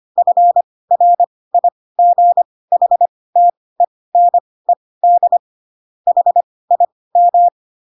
Morse code, heard as a keyed tone, about 25 wpm, spaced as for 18 wpm, 700 Hz. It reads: FRIGHTENED HIM
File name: Pinocchio / Pinocchio ch2 12 25 ik1hge